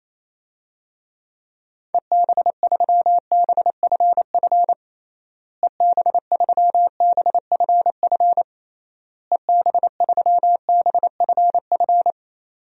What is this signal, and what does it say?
Morse code, heard as a keyed tone, about 27 wpm, 700 Hz. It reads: EB3BFF EB3BFF EB3BFF
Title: DXCC-100 Call Signs - 6 Characters - Repeated 3x - 28wpm